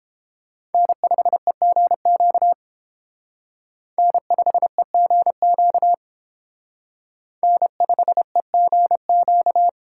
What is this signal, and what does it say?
Morse code, heard as a keyed tone, about 28 wpm, 700 Hz. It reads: N5EGQ N5EGQ N5EGQ